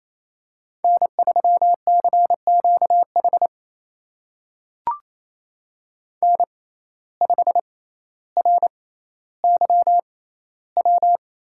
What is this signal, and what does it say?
Morse code, heard as a keyed tone, about 28 wpm, spaced as for 14 wpm, 700 Hz. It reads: N3CQH N5RYW
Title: USA Call Signs - 5 Characters - 6x Character Spacing - 28wpm